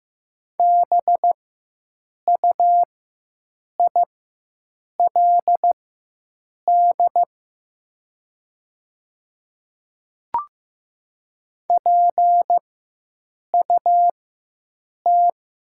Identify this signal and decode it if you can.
Morse code, heard as a keyed tone, about 15 wpm, spaced as for 7 wpm, 700 Hz. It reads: BUILD PUT
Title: 4x Character Spacing - Top 200 Words - 15wpm